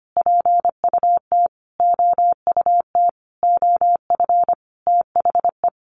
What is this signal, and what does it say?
Morse code, heard as a keyed tone, about 25 wpm, 700 Hz. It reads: PUT OUT OF THE
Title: Sentences from Top 900 Words 25wpm